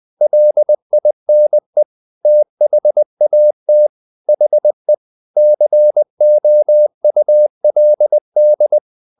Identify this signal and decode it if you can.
Morse code, heard as a keyed tone, about 20 wpm, 600 Hz. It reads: LINE THAT HE COULD
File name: the old man and the sea part 2 20wpm